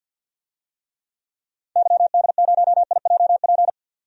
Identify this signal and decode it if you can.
Morse code, heard as a keyed tone, about 50 wpm, 700 Hz. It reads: YD0IJP
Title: DXCC-100 Call Signs - 6 Characters - 50wpm